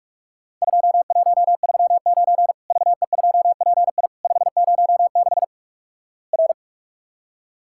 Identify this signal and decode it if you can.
Morse code, heard as a keyed tone, about 46 wpm, 700 Hz. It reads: 2139 VE2PI 506 R
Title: Worked All Europe DX Contest - QTC Exchange - 45wpm